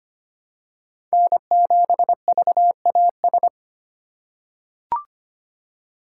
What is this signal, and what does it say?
Morse code, heard as a keyed tone, about 25 wpm, 700 Hz. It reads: N7VAS
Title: USA Call Signs - 5 Characters - 3x Character Spacing - 25wpm